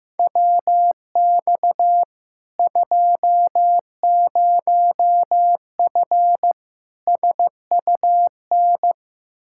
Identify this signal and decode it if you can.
Morse code, heard as a keyed tone, about 16 wpm, 700 Hz. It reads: WX 20F SUN
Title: Practice QSO - Exchange 2 - Info - 15wpm